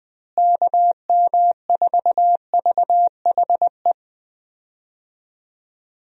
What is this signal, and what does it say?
Morse code, heard as a keyed tone, about 21 wpm, 700 Hz. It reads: KM4VHE
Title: USA Call Signs - 6 Characters - Repeated 2x - 20wpm